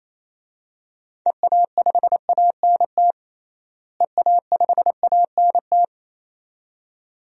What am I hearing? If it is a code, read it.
Morse code, heard as a keyed tone, about 28 wpm, 700 Hz. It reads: EA5ANT EA5ANT
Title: DXCC-100 Call Signs - 6 Characters - Repeated 2x - 28wpm